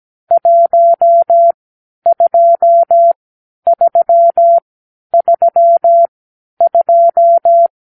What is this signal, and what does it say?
Morse code, heard as a keyed tone, about 17 wpm, spaced as for 10 wpm, 680 Hz. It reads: 12332
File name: k7qo / Lesson 44 (1-3)